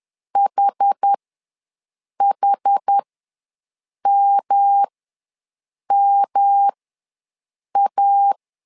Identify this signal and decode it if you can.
Morse code, heard as a keyed tone, about 10 wpm, spaced as for 6 wpm, 800 Hz. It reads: HHMMA
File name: LEZIONI / 3 MWRO / 3 M ws 10wpm 2121